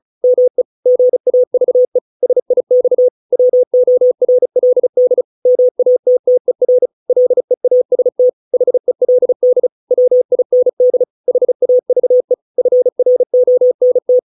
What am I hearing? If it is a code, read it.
Morse code, heard as a keyed tone, about 34 wpm, 500 Hz. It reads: ME GAVE SIX WORLD MATTER LEAST HELD WIND HAVE FRONT